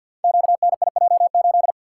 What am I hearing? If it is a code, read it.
Morse code, heard as a keyed tone, about 53 wpm, 700 Hz. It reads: QNIJ8
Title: Code Groups - 5 Letter:Numbers - 50wpm